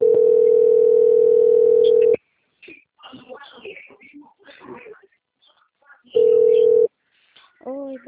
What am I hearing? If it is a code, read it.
Ringback tone (North America), 440 + 480 Hz, with speech.
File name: out- - - - - .51465